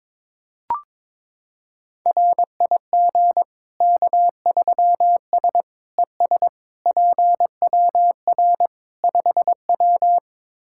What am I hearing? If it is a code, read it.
Morse code, heard as a keyed tone, about 22 wpm, 700 Hz. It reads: RIG K3S ES PWR 5W
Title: K1USN SST QSO Practice 22wpm